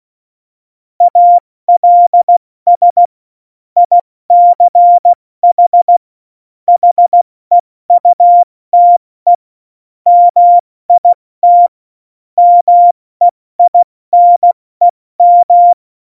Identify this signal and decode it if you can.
Morse code, heard as a keyed tone, about 16 wpm, 700 Hz. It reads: ALS ICH HEUTE MIT MEINEM